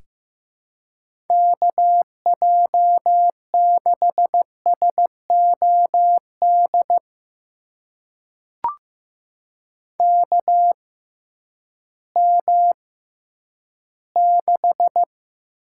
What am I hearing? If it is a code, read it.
Morse code, heard as a keyed tone, about 15 wpm, spaced as for 10 wpm, 700 Hz. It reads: KJ6SOD KM6
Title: USA Call Signs - 6x Character Spacing - 15wpm